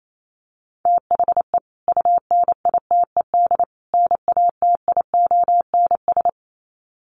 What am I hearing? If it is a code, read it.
Morse code, heard as a keyed tone, about 28 wpm, 700 Hz. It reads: THE UNITED NATIONS
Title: Sets of 3 Words 28wpm